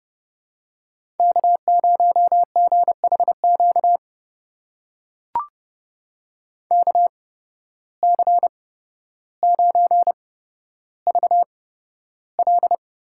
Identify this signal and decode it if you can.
Morse code, heard as a keyed tone, about 30 wpm, spaced as for 13 wpm, 700 Hz. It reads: K0GHQ KC9VL
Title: USA Call Signs - 5 Characters - 8x Character Spacing - 30wpm